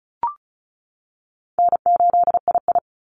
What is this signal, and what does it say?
Morse code, heard as a keyed tone, about 34 wpm, 700 Hz. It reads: N8II
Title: Call Signs - 35wpm